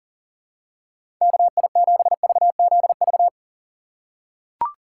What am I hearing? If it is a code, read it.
Morse code, heard as a keyed tone, about 40 wpm, 700 Hz. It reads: KI7VZV
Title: USA Call Signs - 6 Characters - 2x Character Spacing - 40wpm